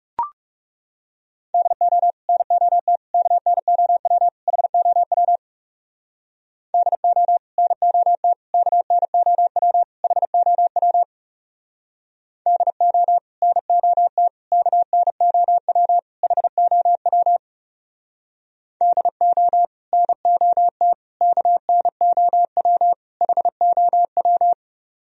Morse code, heard as a keyed tone, about 40 wpm, 700 Hz. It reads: DO NOT KNOW HOW DO NOT KNOW HOW DO NOT KNOW HOW DO NOT KNOW HOW